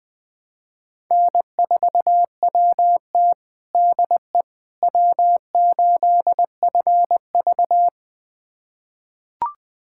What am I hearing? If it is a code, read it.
Morse code, heard as a keyed tone, about 20 wpm, 700 Hz. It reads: N4WT DE W8FV
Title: USA Callsign DE Callsign - 4 Characters - 20wpm